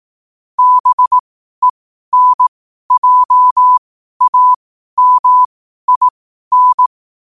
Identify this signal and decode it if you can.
Morse code, heard as a keyed tone, about 18 wpm, spaced as for 12 wpm, 1000 Hz. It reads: BENJAMIN
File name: qotdfiles / 12wpm